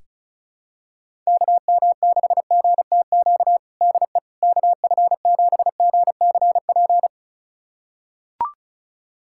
Morse code, heard as a keyed tone, about 34 wpm, 700 Hz. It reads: KM6GTQ DE KF7GCP
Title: USA Callsign DE Callsign - 6 Characters - 35wpm